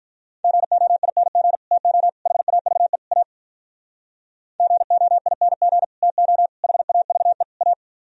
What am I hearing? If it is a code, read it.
Morse code, heard as a keyed tone, about 44 wpm, 700 Hz. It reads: GOING TO HAVE A GOING TO HAVE A